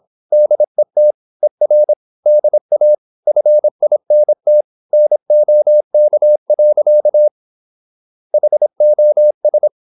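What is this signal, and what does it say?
Morse code, heard as a keyed tone, about 26 wpm, 600 Hz. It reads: DET ER DA FINT NOK. HOS
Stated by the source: High-speed (QRQ) Morse at 26 wpm.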